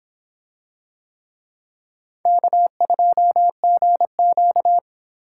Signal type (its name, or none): Morse code